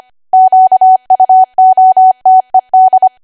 Morse code, heard as a keyed tone, about 25 wpm, 750 Hz. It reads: QUOTED
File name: cwa / W1AW-Level3 / 25WPM